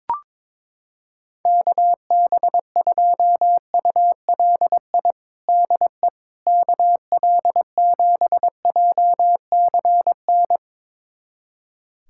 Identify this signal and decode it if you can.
Morse code, heard as a keyed tone, about 21 wpm, 700 Hz. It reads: KB2ULI DE KL7JCN
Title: USA Callsign DE Callsign 22wpm